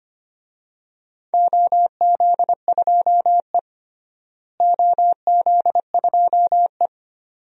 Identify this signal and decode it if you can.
Morse code, heard as a keyed tone, about 25 wpm, 700 Hz. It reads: OZ2E OZ2E
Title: DXCC-100 Call Signs - 4 Characters - Repeated 2x - 25wpm